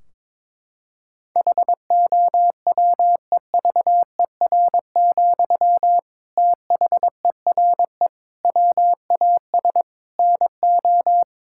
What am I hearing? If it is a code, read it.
Morse code, heard as a keyed tone, about 22 wpm, 700 Hz. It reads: HOWEVER, THERE WAS NO
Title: Sentences from Top 1600 Words 22wpm